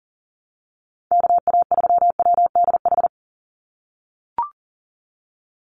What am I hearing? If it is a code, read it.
Morse code, heard as a keyed tone, about 40 wpm, 700 Hz. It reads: KA3WDH
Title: USA Call Signs 40wpm